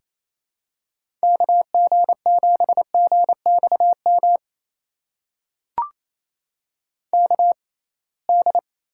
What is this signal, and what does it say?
Morse code, heard as a keyed tone, about 28 wpm, spaced as for 21 wpm, 700 Hz. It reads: KG7GXM KD
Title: USA Call Signs - 6 Characters - 6x Character Spacing - 28wpm